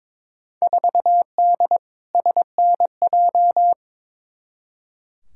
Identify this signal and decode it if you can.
Morse code, heard as a keyed tone, about 22 wpm, 700 Hz. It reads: 4D SNJ
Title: Field Day Exchanges Encore - 22wpm